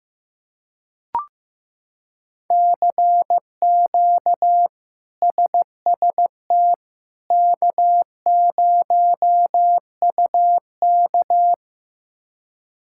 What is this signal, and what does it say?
Morse code, heard as a keyed tone, about 15 wpm, 700 Hz. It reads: CQ SST K0UK
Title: K1USN SST QSO Practice - 15wpm